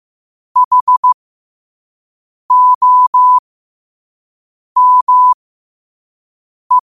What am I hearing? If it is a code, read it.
Morse code, heard as a keyed tone, about 15 wpm, spaced as for 5 wpm, 1000 Hz. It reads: HOME
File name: qotdfiles / 5wpm